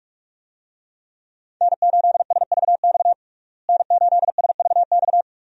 Morse code, heard as a keyed tone, about 46 wpm, 700 Hz. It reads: N8SVX N8SVX
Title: USA Call Signs - 5 Characters - Repeated 2x - 45wpm